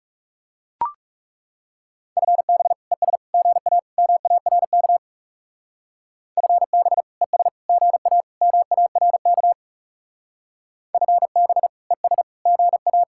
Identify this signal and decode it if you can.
Morse code, heard as a keyed tone, about 39 wpm, 700 Hz. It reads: FB ES GA MARK FB ES GA MARK FB ES GA